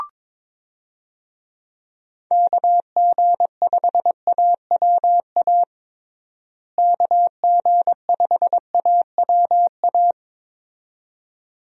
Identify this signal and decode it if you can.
Morse code, heard as a keyed tone, about 22 wpm, 700 Hz. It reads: KG5AWA KG5AWA
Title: USA Call Signs - 6 Characters - Repeated 2x - 22wpm